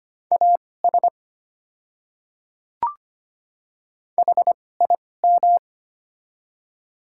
Morse code, heard as a keyed tone, about 25 wpm, spaced as for 18 wpm, 700 Hz. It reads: AS HIM
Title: 2x Character Spacing - Top 100 Words - 25wpm